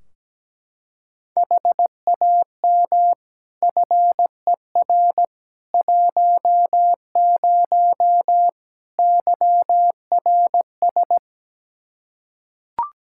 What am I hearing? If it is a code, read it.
Morse code, heard as a keyed tone, about 17 wpm, 700 Hz. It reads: HAM FER 10 YRS
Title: Practice QSO - Exchange 3 - More Info - 17wpm